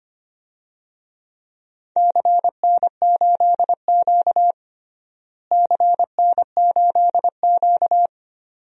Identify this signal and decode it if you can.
Morse code, heard as a keyed tone, about 25 wpm, 700 Hz. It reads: CN8Q CN8Q
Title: DXCC-100 Call Signs - 4 Characters - Repeated 2x - 25wpm